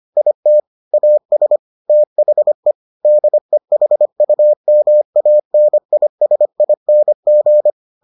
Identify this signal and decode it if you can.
Morse code, heard as a keyed tone, about 25 wpm, 600 Hz. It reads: IT AS THE DEHUMANISING